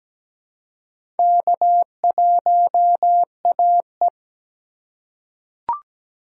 Morse code, heard as a keyed tone, about 17 wpm, 700 Hz. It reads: K1AE
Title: USA Call Signs - 4 Characters - 2x Character Spacing - 17wpm